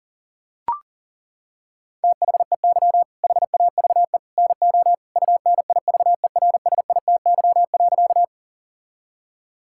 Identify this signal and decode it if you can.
Morse code, heard as a keyed tone, about 40 wpm, 700 Hz. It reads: THEY HAVE NO UNIVERSITY.
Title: Sentences from Top 1000 Words - Review Part 5 - 40wpm